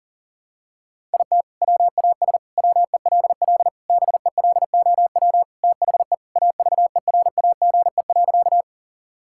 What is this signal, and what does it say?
Morse code, heard as a keyed tone, about 37 wpm, 700 Hz. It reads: IT WAS WELL BELOW THE AVERAGE.
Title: Sentences from Top 1000 Words - Review Part 6 - 40wpm